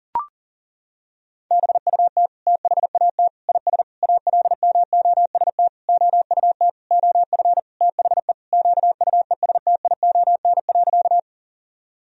Morse code, heard as a keyed tone, about 40 wpm, 700 Hz. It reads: BUT THAT IS ALMOST OUT OF THE QUESTION.
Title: Sentences from Top 1000 Words - Review Part 4 - 40wpm